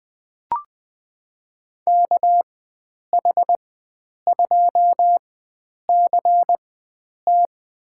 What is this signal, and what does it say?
Morse code, heard as a keyed tone, about 20 wpm, spaced as for 9 wpm, 700 Hz. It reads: KH2CT
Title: USA Call Signs - 4x Character Spacing - 20wpm